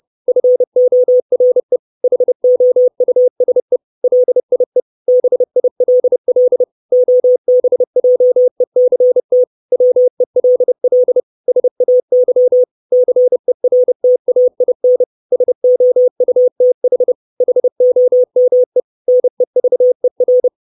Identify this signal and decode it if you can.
Morse code, heard as a keyed tone, about 30 wpm, 500 Hz. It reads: FORE HOUSE LIE BILL OBJECT WELL SAY CERTAIN SOUTH HOME NEVER
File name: 632 Words 30WPM